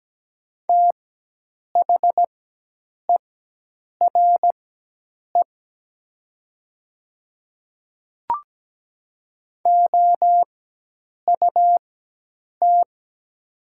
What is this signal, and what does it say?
Morse code, heard as a keyed tone, about 17 wpm, spaced as for 8 wpm, 700 Hz. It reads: THERE OUT
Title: 4x Character Spacing - Top 100 Words - 17wpm